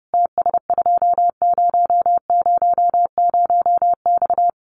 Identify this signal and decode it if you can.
Morse code, heard as a keyed tone, about 28 wpm, 700 Hz. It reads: TS2000X
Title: Common QSO Phrases 30wpm